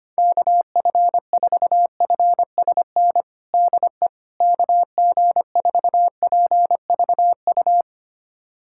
Morse code, heard as a keyed tone, about 25 wpm, 700 Hz. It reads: KF4FSN DE KG4PVU
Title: USA Callsign DE Callsign - 6 Characters - 25wpm